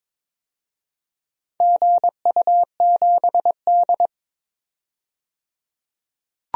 Morse code, heard as a keyed tone, about 23 wpm, 700 Hz. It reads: GU7D
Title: DXCC-100 Call Signs - 4 Characters - 22wpm